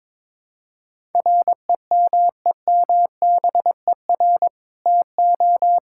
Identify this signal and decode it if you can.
Morse code, heard as a keyed tone, about 23 wpm, 700 Hz. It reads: REMEMBER TO
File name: practice / 22wpm / Sentences with 2-Word Phrases - 22wpm